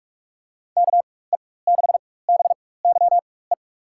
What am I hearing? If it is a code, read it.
Morse code, heard as a keyed tone, about 44 wpm, spaced as for 21 wpm, 700 Hz. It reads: KE6BYE
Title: USA Call Signs - 6 Characters - 4x Character Spacing - 45wpm